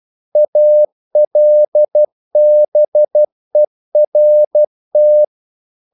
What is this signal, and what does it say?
Morse code, heard as a keyed tone, about 12 wpm, 600 Hz. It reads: ALBERT
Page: English quotations in CW (telegraphy - morse)